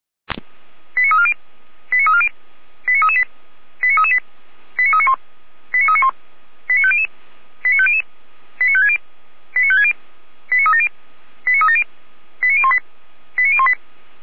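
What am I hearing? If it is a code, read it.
Five-tone selective call (ZVEI), heard as quick runs of five tones, tones of about 70 ms. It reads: 89340 89340 89338 89338 89391 89391 89500 89500 89560 89560 89370 89370 89018 89018
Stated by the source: ZVEI five-tone selective call received on an SDR.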